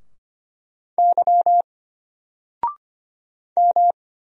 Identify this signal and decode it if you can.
Morse code, heard as a keyed tone, about 25 wpm, 700 Hz. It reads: Y M
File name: Single-Letter-Number-25wpm